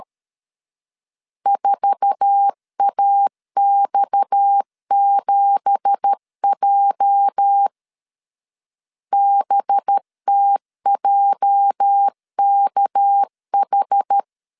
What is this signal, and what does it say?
Morse code, heard as a keyed tone, about 13 wpm, 800 Hz. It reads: 4AX7J BTJKH
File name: CW / DET / 1 9 1500C S B 12wpm 3819